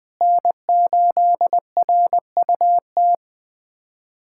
Morse code, heard as a keyed tone, about 20 wpm, 700 Hz. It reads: N8RUT
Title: USA Callsign DE Callsign - 5 Characters - 20wpm